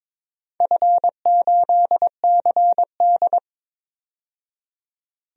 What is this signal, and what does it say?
Morse code, heard as a keyed tone, about 23 wpm, 700 Hz. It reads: F8CD
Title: DXCC-100 Call Signs - 4 Characters - 22wpm